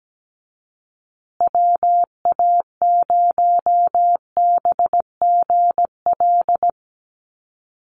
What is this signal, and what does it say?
Morse code, heard as a keyed tone, about 17 wpm, 700 Hz. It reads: WA0BGL
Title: USA Call Signs Repeated 3x 17wpm